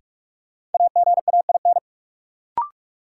Morse code, heard as a keyed tone, about 42 wpm, 700 Hz. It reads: AGAIN
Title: Sets of 4 Words - New Word Drill - 45wpm